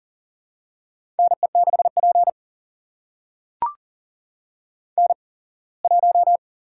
Morse code, heard as a keyed tone, about 39 wpm, spaced as for 24 wpm, 700 Hz. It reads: NE6P N1
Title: USA Call Signs - 8x Character Spacing - 40wpm